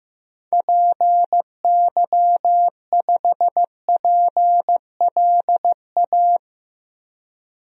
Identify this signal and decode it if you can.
Morse code, heard as a keyed tone, about 15 wpm, 700 Hz. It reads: PY5PLA
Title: DXCC-100 Call Signs - 6 Characters - 15wpm